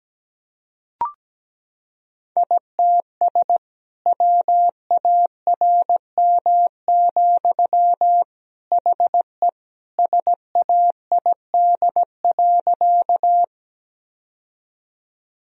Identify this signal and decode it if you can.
Morse code, heard as a keyed tone, about 17 wpm, 700 Hz. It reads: ITS WARM, HE SAID.
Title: Sentences from Top 1000 Words - Review Part 2 - 17wpm